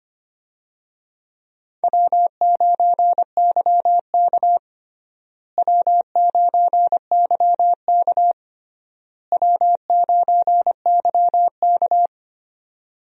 Morse code, heard as a keyed tone, about 25 wpm, 700 Hz. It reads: W9YK W9YK W9YK
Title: USA Call Signs - 4 Characters - Repeated 3x - 25wpm